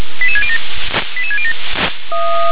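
Five-tone selective call (ZVEI), heard as quick runs of five tones, tones of about 70 ms. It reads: 88607 88607
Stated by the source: ZVEI five-tone selective call received on SDR.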